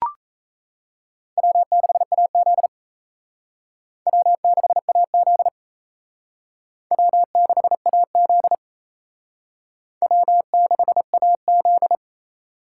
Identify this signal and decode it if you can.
Morse code, heard as a keyed tone, about 36 wpm, 700 Hz. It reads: W6AZ W6AZ W6AZ W6AZ